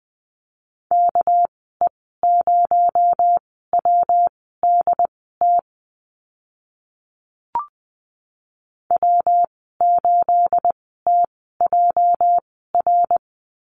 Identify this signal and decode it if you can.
Morse code, heard as a keyed tone, about 20 wpm, spaced as for 14 wpm, 700 Hz. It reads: KE0WDT W8TJR
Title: 245 Lesson 39g - Letter z - Call Signs 20wpm